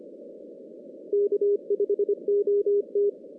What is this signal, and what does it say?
Morse code, heard as a keyed tone, about 25 wpm, 410 Hz. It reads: K5OT
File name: 2016 JAN CW NAQP / 2016 Jan naqp k5ot - 0-10